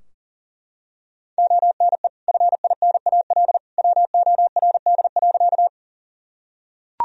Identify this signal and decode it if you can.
Morse code, heard as a keyed tone, about 40 wpm, 700 Hz. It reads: ONE FINAL WORD.
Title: Sentences from Top 1000 Words - Review Part 2 - 40wpm